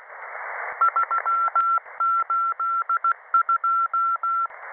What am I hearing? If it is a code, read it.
Morse code, heard as a keyed tone, about 16 wpm, 1350 Hz. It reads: 382
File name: websdr recording 6891.0kHz